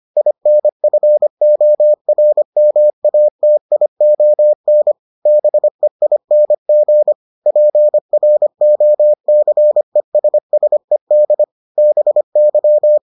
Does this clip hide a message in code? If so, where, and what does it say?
Yes: Morse code from 0.2 to 13.0 s, heard as a keyed tone, about 25 wpm, 600 Hz. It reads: INFORMATION BEING PROCESSED BY